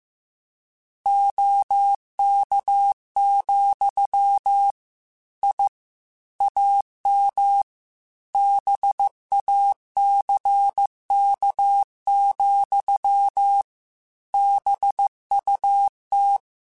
Morse code, heard as a keyed tone, about 15 wpm, 800 Hz. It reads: OK, I AM BACK, BUT